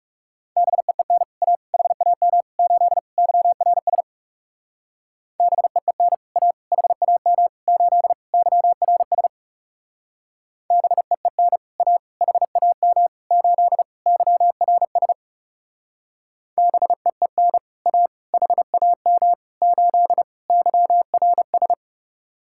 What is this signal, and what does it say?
Morse code, heard as a keyed tone, about 37 wpm, 700 Hz. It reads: BEEN A HAM 8 YRS BEEN A HAM 8 YRS BEEN A HAM 8 YRS BEEN A HAM 8 YRS
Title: Speed-Racing Common QSO Phrases - 30wpm